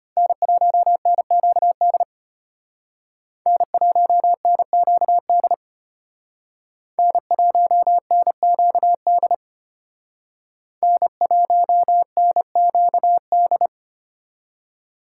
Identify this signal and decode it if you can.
Morse code, heard as a keyed tone, about 32 wpm, 700 Hz. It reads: N1NQD N1NQD N1NQD N1NQD